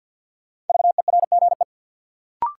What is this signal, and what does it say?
Morse code, heard as a keyed tone, about 48 wpm, 700 Hz. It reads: VERGE
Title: Sets of 4 Words - New Word Drill - 50wpm